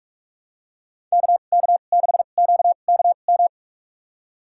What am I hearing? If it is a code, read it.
Morse code, heard as a keyed tone, about 47 wpm, spaced as for 33 wpm, 700 Hz. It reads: KK6QKM
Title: USA Call Signs - 6 Characters - 2x Character Spacing - 45wpm